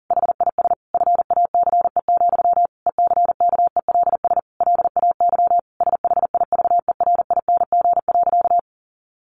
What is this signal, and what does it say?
Morse code, heard as a keyed tone, about 40 wpm, 700 Hz. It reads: HIS FACE, ECKELS LAY SHIVERING.
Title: A Sound Of Thunder 40wpm